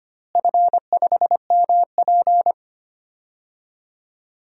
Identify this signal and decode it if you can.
Morse code, heard as a keyed tone, about 25 wpm, 700 Hz. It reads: F5MP
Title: DXCC-100 Call Signs - 4 Characters - Repeated 2x - 25wpm